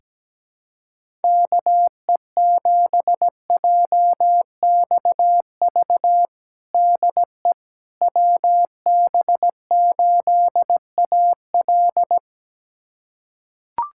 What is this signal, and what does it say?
Morse code, heard as a keyed tone, about 17 wpm, 700 Hz. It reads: KE7JXV DE WB8AL
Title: USA Callsign DE Callsign 17wpm